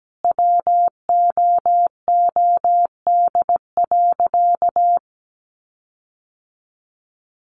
Morse code, heard as a keyed tone, about 17 wpm, 700 Hz. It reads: WOOD.
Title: Sentences from Top 1300 Words 17wpm